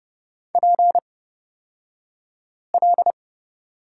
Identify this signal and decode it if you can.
Morse code, heard as a keyed tone, about 30 wpm, spaced as for 5 wpm, 700 Hz. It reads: PL
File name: Lesson 36d - BT - Two Characters 30wpm